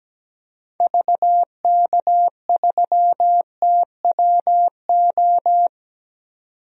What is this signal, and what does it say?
Morse code, heard as a keyed tone, about 17 wpm, 700 Hz. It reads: VK3TWO
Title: DXCC-100 Call Signs - 6 Characters - 17wpm